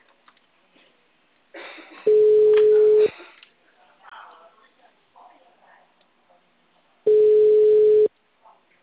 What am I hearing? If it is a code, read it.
Ringback tone (Europe, 425 Hz), with speech.
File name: out- -2050- - - .17537